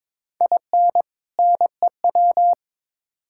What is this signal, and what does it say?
Morse code, heard as a keyed tone, about 22 wpm, 700 Hz. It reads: IN NEW
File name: Build A Phrase - 2-Word Phrases - 22wpm